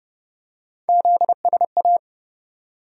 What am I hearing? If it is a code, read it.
Morse code, heard as a keyed tone, about 30 wpm, 700 Hz. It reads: ZSA
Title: USA Call Sign Postfixes - 3 Letters 30wpm